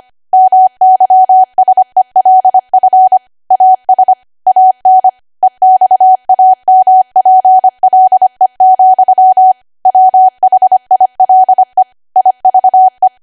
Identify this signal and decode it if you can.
Morse code, heard as a keyed tone, about 25 wpm, 750 Hz. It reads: MYSELF AS AN EXAMPLE, WHILE IVE
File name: Archive / 25 WPM / 25WPM